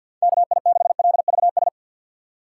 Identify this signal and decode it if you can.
Morse code, heard as a keyed tone, about 50 wpm, 700 Hz. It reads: KI6LVS
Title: USA Call Signs - 6 Characters - 6x Character Spacing - 50wpm